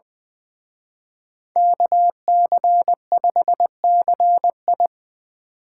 Morse code, heard as a keyed tone, about 24 wpm, 700 Hz. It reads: KC5CI
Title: Speed Racing - USA 2x2 Call Signs 20wpm